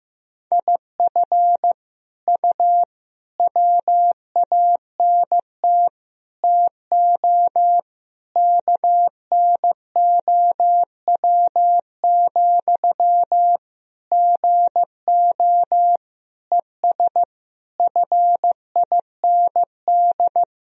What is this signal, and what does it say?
Morse code, heard as a keyed tone, about 15 wpm, 700 Hz. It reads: IF U WANT TO KNOW, GO ES FIND
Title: Sentences from Top 200 Words and Abbreviations 15wpm